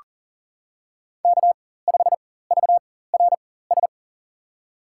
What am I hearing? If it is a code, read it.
Morse code, heard as a keyed tone, about 40 wpm, spaced as for 19 wpm, 700 Hz. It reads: K5VRS
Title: USA Call Signs - 5 Characters - 4x Character Spacing - 40wpm